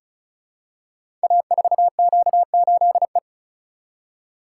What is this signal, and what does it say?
Morse code, heard as a keyed tone, about 36 wpm, 690 Hz. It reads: A4Q8E
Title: Code Groups - 5 Letter:Numbers - 35wpm